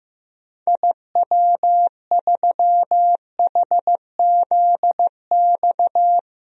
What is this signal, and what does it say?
Morse code, heard as a keyed tone, about 16 wpm, 700 Hz. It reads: IW3HZX
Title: DXCC-100 Call Signs - 6 Characters - 15wpm